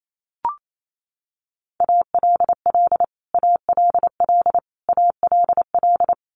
Morse code, heard as a keyed tone, about 28 wpm, 700 Hz. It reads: ALL ALL ALL
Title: Top 200 Words Repeated 28wpm